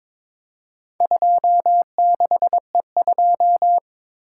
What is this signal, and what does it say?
Morse code, heard as a keyed tone, about 22 wpm, 700 Hz. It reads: 26E2
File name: practice / 22wpm / Four Letter-Number Combinations - 22wpm